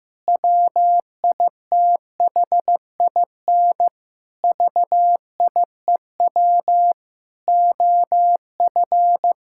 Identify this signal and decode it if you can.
Morse code, heard as a keyed tone, about 15 wpm, 700 Hz. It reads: WITHIN VIEW OF
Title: Sentences from Top 1000 Words - Review Part 6 - 15wpm